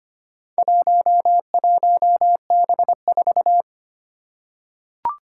Morse code, heard as a keyed tone, about 25 wpm, 700 Hz. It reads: 11B4
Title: Four Challenging Letter-Number Combinations 25wpm